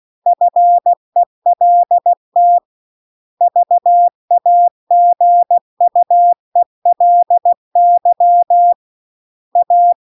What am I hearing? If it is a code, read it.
Morse code, heard as a keyed tone, about 16 wpm, 700 Hz. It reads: FELT VAGUELY A